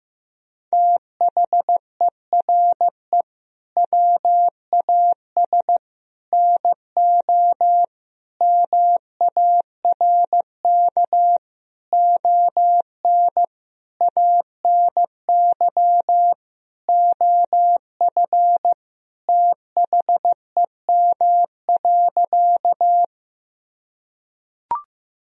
Morse code, heard as a keyed tone, about 15 wpm, 700 Hz. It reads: THERE WAS NO MARK ON ANY OF THEM.
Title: Sentences from Top 1000 Words - Review Part 1 - 15wpm